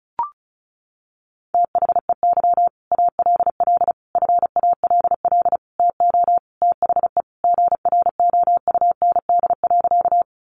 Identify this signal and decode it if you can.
Morse code, heard as a keyed tone, about 34 wpm, 700 Hz. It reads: THEY ALL FALL TO THE GROUND.
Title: Sentences from Top 500 Words 35wpm